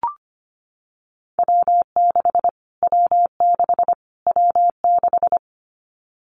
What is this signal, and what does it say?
Morse code, heard as a keyed tone, about 25 wpm, 700 Hz. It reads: W6 W6 W6
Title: Callsign Prefixes for Top 25 DXCC - Repeated 3x - 25wpm